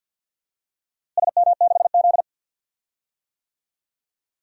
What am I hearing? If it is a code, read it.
Morse code, heard as a keyed tone, about 51 wpm, 700 Hz. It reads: SM6Z
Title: DXCC-100 Call Signs - 4 Characters - Repeated 2x - 50wpm